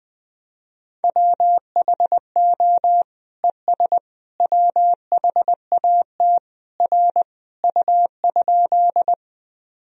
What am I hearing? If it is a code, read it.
Morse code, heard as a keyed tone, about 21 wpm, 700 Hz. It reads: WHO ES WHAT R U?
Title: Sentences from Top 100 Words and Abbreviations 20wpm